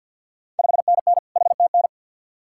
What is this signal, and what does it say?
Morse code, heard as a keyed tone, about 47 wpm, 690 Hz. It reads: 5NN HTN
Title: Ontario QSO Party Exchange 50wpm